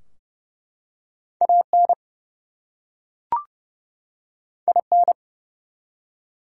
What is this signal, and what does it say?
Morse code, heard as a keyed tone, about 30 wpm, 700 Hz. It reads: AN IN
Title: Two Letter Words - 30wpm